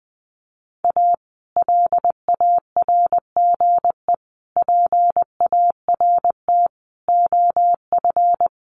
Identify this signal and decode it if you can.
Morse code, heard as a keyed tone, about 20 wpm, 700 Hz. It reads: A LARGE PART OF
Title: Sets of 5 Words Encore2 20wpm